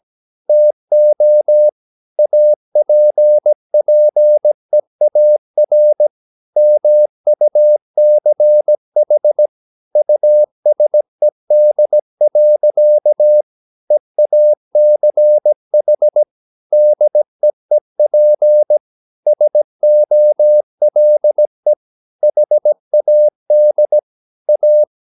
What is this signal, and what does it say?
Morse code, heard as a keyed tone, about 17 wpm, 600 Hz. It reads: TO APPEAR MUCH USED. EACH DEEP SOLE HAD A
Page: Romanen Never af Ken Follet på engelsk.